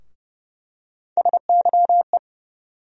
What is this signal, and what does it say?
Morse code, heard as a keyed tone, about 30 wpm, 700 Hz. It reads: SYE
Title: USA Call Sign Postfixes - 3 Letters 30wpm